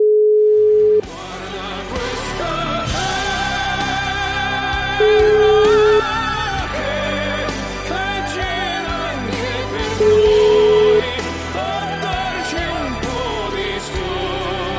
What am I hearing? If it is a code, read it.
Ringback tone (Europe, 425 Hz), with noise.